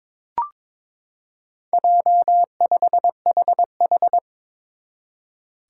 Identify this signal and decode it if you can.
Morse code, heard as a keyed tone, about 21 wpm, 700 Hz. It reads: J5HH
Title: Four Challenging Letter-Number Combinations 22wpm